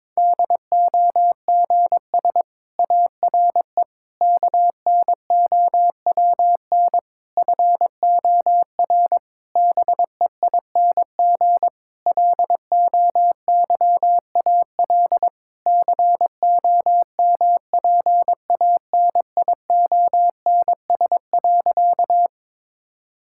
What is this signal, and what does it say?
Morse code, heard as a keyed tone, about 22 wpm, 700 Hz. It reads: DOGS ARE KNOWN FOR BEING LOYAL COMPANIONS.
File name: Difficult Sentences with Hints - 22wpm